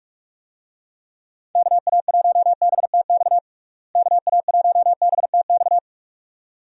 Morse code, heard as a keyed tone, about 43 wpm, 700 Hz. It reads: KA1BTX KA1BTX
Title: USA Call Signs - 6 Characters - Repeated 2x - 45wpm